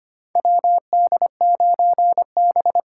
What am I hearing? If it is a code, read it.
Morse code, heard as a keyed tone, about 25 wpm, 700 Hz. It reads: WD9B
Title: USA Call Signs - 6 Characters - 6x Character Spacing - 25wpm